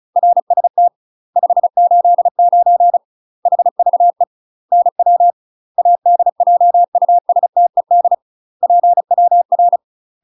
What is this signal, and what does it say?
Morse code, heard as a keyed tone, about 35 wpm, 700 Hz. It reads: RST 589 HVE NW ADJUSTED PWR
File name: UK MASTER QSOs TRAINING / 35WPM / UK MASTER QSOs 0022